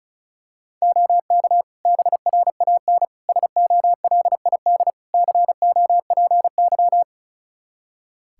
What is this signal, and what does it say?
Morse code, heard as a keyed tone, about 34 wpm, 700 Hz. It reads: OK BRAN SOLID COPY
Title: Practice QSO - Exchange 3 - More Info - 35wpm